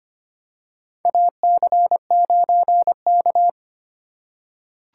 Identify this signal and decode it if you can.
Morse code, heard as a keyed tone, about 25 wpm, 700 Hz. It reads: AC9K
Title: USA Call Signs - 4 Characters - 3x Character Spacing - 25wpm